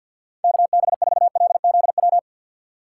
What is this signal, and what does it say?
Morse code, heard as a keyed tone, about 50 wpm, 700 Hz. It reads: KB4LZW
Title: USA Call Signs - 6 Characters - 4x Character Spacing - 50wpm